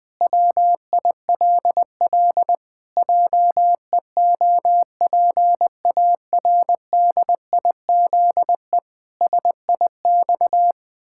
Morse code, heard as a keyed tone, about 20 wpm, 700 Hz. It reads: WILL JEOPARDIZE SIX
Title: Pangram Challenge v2 - 20wpm